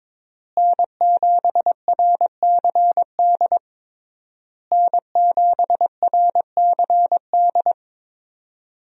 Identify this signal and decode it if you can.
Morse code, heard as a keyed tone, about 22 wpm, 700 Hz. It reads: N7RCD N7RCD
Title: USA Call Signs - 5 Characters - Repeated 3x - 22wpm